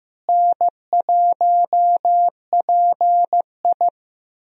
Morse code, heard as a keyed tone, about 15 wpm, 700 Hz. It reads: N1PI
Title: USA Call Signs - 4 Characters - Repeated 2x - 15wpm